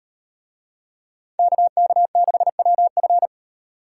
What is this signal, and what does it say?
Morse code, heard as a keyed tone, about 38 wpm, 700 Hz. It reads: KK6WF